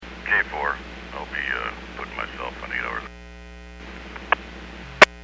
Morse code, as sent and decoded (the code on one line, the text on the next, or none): none